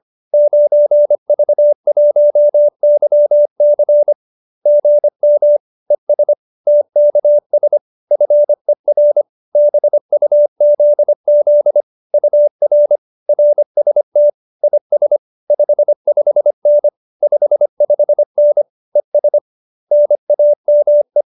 Morse code, heard as a keyed tone, about 25 wpm, 590 Hz. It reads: 9V1YC GM ES TKS FER BUZZ UR RST IS 55N 55N ES NAME
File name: qso207b 25